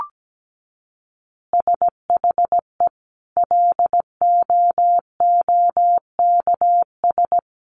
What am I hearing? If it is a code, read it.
Morse code, heard as a keyed tone, about 17 wpm, 700 Hz. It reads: SHE LOOKS
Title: Additional Easy Phrases 17wpm